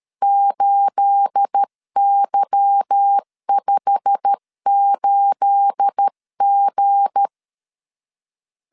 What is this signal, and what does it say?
Morse code, heard as a keyed tone, about 13 wpm, 800 Hz. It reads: 8Y58G
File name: LEZIONI / 9 28 12wpm 2351